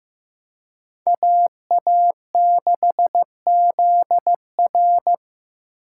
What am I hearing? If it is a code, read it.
Morse code, heard as a keyed tone, about 18 wpm, 700 Hz. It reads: AA6ZR
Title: Speed Racing - USA Call Signs 15wpm